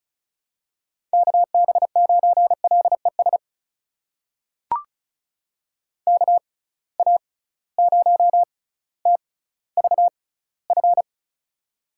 Morse code, heard as a keyed tone, about 36 wpm, spaced as for 18 wpm, 700 Hz. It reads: KB9LES KA0TVF
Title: USA Call Signs - 6 Characters - 6x Character Spacing - 35wpm